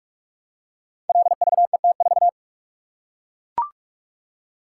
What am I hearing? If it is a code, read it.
Morse code, heard as a keyed tone, about 44 wpm, 700 Hz. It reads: LVET4
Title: Code Groups - 5 Letter:Numbers - 45wpm